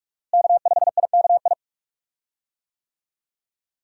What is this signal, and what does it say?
Morse code, heard as a keyed tone, about 46 wpm, 700 Hz. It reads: K5IKI
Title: USA Call Signs - 5 Characters - Repeated 3x - 45wpm